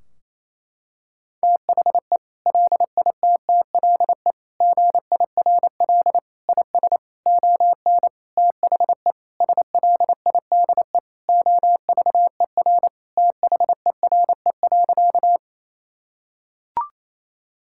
Morse code, heard as a keyed tone, about 27 wpm, 700 Hz. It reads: THE LITTLE GIRL IS ON THE SLIDE OVER THERE.
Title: Sentences from Top 1600 Words 28wpm